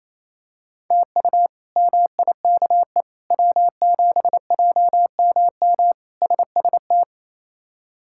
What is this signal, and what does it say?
Morse code, heard as a keyed tone, about 28 wpm, 700 Hz. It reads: TU MIKE W7JMM SST
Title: K1USN SST QSO Practice - 28wpm